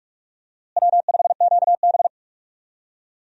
Morse code, heard as a keyed tone, about 43 wpm, 700 Hz. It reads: W5QB
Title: USA Call Signs - 4 Characters - Repeated 3x - 45wpm